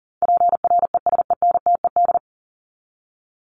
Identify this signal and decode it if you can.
Morse code, heard as a keyed tone, about 40 wpm, 700 Hz. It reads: PRESENTED
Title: ED Suffix - Top 100 Words with Base Words 40wpm